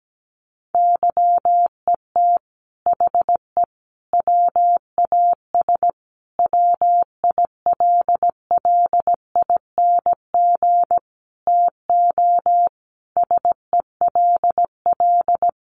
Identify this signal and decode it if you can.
Morse code, heard as a keyed tone, about 17 wpm, 700 Hz. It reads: YET HE WAS WILLING TO SELL
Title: Sentences from Top 1100 Words 17wpm